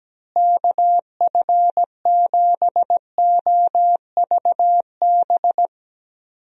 Morse code, heard as a keyed tone, about 17 wpm, 700 Hz. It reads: KF7OVB
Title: USA Call Signs - 6 Characters - Repeated 2x - 17wpm